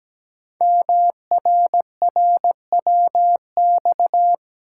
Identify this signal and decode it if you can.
Morse code, heard as a keyed tone, about 17 wpm, 700 Hz. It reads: MRRWX
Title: Code Groups - 5 Letters - 17wpm